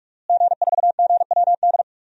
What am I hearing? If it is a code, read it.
Morse code, heard as a keyed tone, about 46 wpm, 700 Hz. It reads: G4GWD
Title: DXCC-100 Call Signs - 5 Characters - Repeated 2x - 45wpm